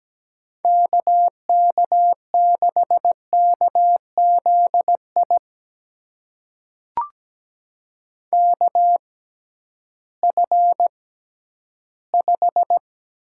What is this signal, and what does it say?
Morse code, heard as a keyed tone, about 17 wpm, spaced as for 11 wpm, 700 Hz. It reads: KK6KZI KF5
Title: USA Call Signs - 6 Characters - 6x Character Spacing - 17wpm